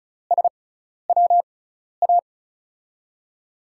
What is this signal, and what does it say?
Morse code, heard as a keyed tone, about 36 wpm, spaced as for 12 wpm, 700 Hz. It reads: SWA
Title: USA Call Signs - 6 Characters - 6x Character Spacing - 35wpm